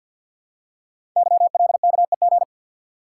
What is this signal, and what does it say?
Morse code, heard as a keyed tone, about 51 wpm, 700 Hz. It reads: YLKEG
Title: Code Groups - 5 Letters - 50wpm